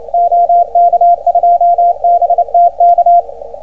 Morse code, heard as a keyed tone, about 27 wpm, 690 Hz. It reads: OK2BTK